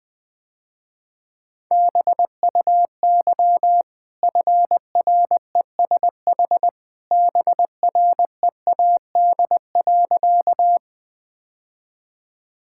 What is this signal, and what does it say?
Morse code, heard as a keyed tone, about 20 wpm, 700 Hz. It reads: BUY FRESH BREAD.
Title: Easy Sentences with Hints - 20wpm